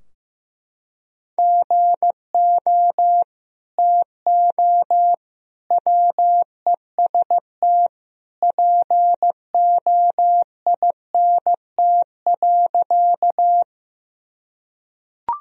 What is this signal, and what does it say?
Morse code, heard as a keyed tone, about 15 wpm, 700 Hz. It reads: GO TO WEST POINT.
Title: Sentences from Top 1000 Words - Review Part 2 - 15wpm